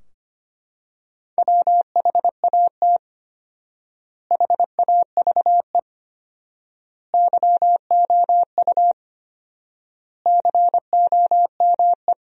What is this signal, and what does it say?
Morse code, heard as a keyed tone, about 25 wpm, 700 Hz. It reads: WHAT HAVE YOU COME
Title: 4x Word-Spacing - Sentences from Top 100 Words - 25wpm